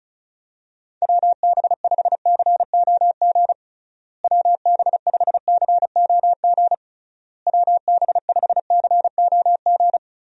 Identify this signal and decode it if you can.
Morse code, heard as a keyed tone, about 36 wpm, 700 Hz. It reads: WB5COG WB5COG WB5COG
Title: USA Call Signs - 6 Characters - Repeated 3x - 35wpm